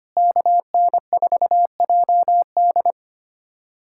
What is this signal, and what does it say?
Morse code, heard as a keyed tone, about 25 wpm, 700 Hz. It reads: KN4JD
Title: USA Callsign DE Callsign - 5 Characters - 25wpm